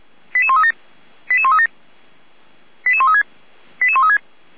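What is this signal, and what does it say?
Five-tone selective call (ZVEI), heard as quick runs of five tones, tones of about 70 ms. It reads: 88137 88137 88136 88136
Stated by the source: ZVEI five-tone selective call received via SDR.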